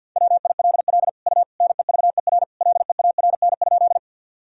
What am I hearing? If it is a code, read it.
Morse code, heard as a keyed tone, about 50 wpm, 700 Hz. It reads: WILL U NEVER LEARN?
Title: Sentences from Top 300 Words and Abbreviations 50wpm